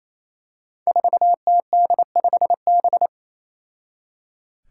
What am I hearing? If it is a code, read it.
Morse code, heard as a keyed tone, about 28 wpm, 700 Hz. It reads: 4TD5B
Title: Five Challenging Letter-Number Combinations 28wpm